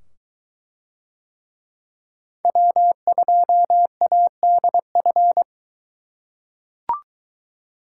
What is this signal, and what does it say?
Morse code, heard as a keyed tone, about 22 wpm, 700 Hz. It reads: W2ADF